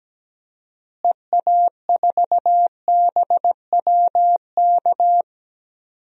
Morse code, heard as a keyed tone, about 17 wpm, 700 Hz. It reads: EA4BWK
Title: DXCC-100 Call Signs - 6 Characters - Repeated 2x - 17wpm